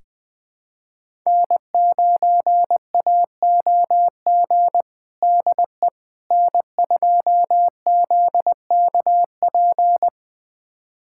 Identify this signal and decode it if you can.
Morse code, heard as a keyed tone, about 21 wpm, 700 Hz. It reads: N9AOG DE N2ZKP
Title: USA Callsign DE Callsign - 5 Characters - 20wpm